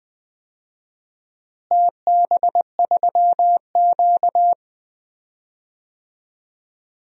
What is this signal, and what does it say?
Morse code, heard as a keyed tone, about 20 wpm, 700 Hz. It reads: TB3Q
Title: DXCC-100 Call Signs - 4 Characters - Repeated 2x - 20wpm